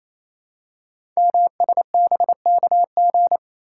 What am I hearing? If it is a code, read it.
Morse code, heard as a keyed tone, about 29 wpm, 700 Hz. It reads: MSBKG
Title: Code Groups - 5 Letters - 28wpm